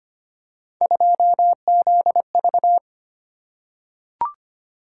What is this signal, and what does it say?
Morse code, heard as a keyed tone, about 25 wpm, 700 Hz. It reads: 2ZV
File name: Three Letter-Number Combinations - 25wpm